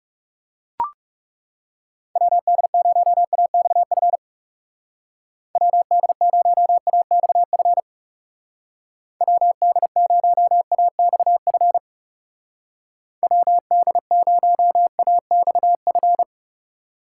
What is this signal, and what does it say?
Morse code, heard as a keyed tone, about 36 wpm, 700 Hz. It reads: WD0AXF WD0AXF WD0AXF WD0AXF
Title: Speed Racing - USA Call Signs 30wpm